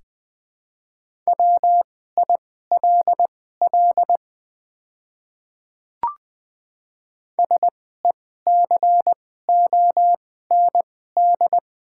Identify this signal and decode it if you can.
Morse code, heard as a keyed tone, about 21 wpm, spaced as for 15 wpm, 700 Hz. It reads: WILL SECOND
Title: 2x Character Spacing - Top 500 Words - 20wpm